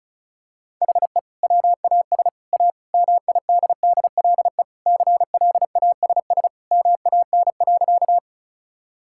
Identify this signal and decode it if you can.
Morse code, heard as a keyed tone, about 36 wpm, 700 Hz. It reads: HE WAS A MIDDLE CLASS MAN.
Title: Sentences from Top 1000 Words - Review Part 6 - 35wpm